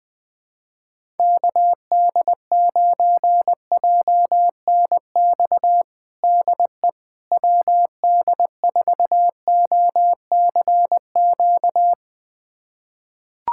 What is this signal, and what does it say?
Morse code, heard as a keyed tone, about 20 wpm, 700 Hz. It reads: KD9JNX DE WD4OCQ
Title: USA Callsign DE Callsign 20wpm